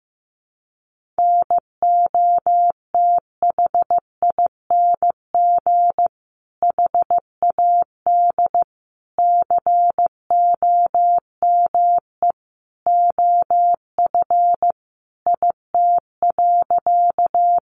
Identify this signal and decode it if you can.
Morse code, heard as a keyed tone, about 14 wpm, 700 Hz. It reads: NOTHING HAD COME OF IT.
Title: Sentences from Top 500 Words 15wpm